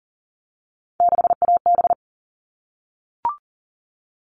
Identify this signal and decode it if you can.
Morse code, heard as a keyed tone, about 40 wpm, 700 Hz. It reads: 6AB
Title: All Cut Numbers - 3-Digit 40wpm